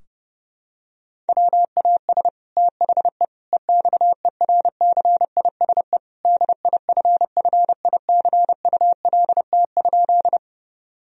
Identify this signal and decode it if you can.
Morse code, heard as a keyed tone, about 30 wpm, 700 Hz. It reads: WAS THE EXERCISE DIFFICULT?
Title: Sentences from Top 1000 Words - Review Part 4 - 30wpm